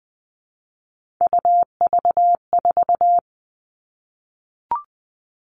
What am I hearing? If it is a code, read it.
Morse code, heard as a keyed tone, about 20 wpm, 700 Hz. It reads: UV4
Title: All Cut Numbers - 3-Digit 20wpm